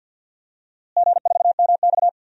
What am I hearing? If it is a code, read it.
Morse code, heard as a keyed tone, about 51 wpm, 700 Hz. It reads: G4MX
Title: DXCC-100 Call Signs - 4 Characters - Repeated 3x - 50wpm